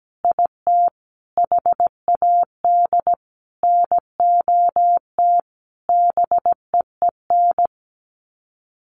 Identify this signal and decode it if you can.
Morse code, heard as a keyed tone, about 17 wpm, 700 Hz. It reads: IT HAD NOT BEEN
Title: Sets of 5 Words Encore 17wpm